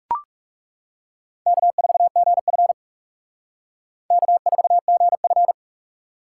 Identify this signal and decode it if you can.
Morse code, heard as a keyed tone, about 37 wpm, 700 Hz. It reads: K4GF K4GF
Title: Speed Racing - USA 1x2 Call Signs 30wpm